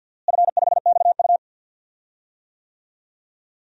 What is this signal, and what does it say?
Morse code, heard as a keyed tone, about 47 wpm, 700 Hz. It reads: F5XU
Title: DXCC-100 Call Signs - 4 Characters - Repeated 2x - 50wpm